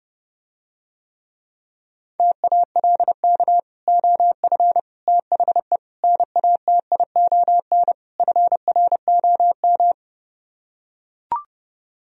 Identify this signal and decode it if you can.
Morse code, heard as a keyed tone, about 32 wpm, 700 Hz. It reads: TALK OF THE NATION FROM